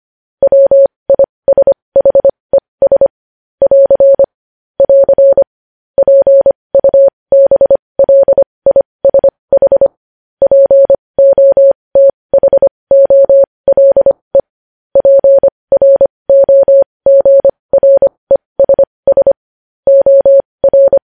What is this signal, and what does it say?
Morse code, heard as a keyed tone, about 25 wpm, spaced as for 20 wpm, 550 Hz. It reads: WISHES <AR> <AR> PUBLISH POTHOLE PROGRESS OR
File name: podcast / News headlines-1x-25x20